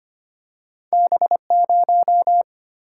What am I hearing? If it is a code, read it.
Morse code, heard as a keyed tone, about 25 wpm, 700 Hz. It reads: B0
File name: Two Letter-Number Combinations - 25wpm